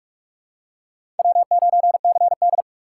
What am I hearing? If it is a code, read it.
Morse code, heard as a keyed tone, about 39 wpm, 700 Hz. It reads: W9CD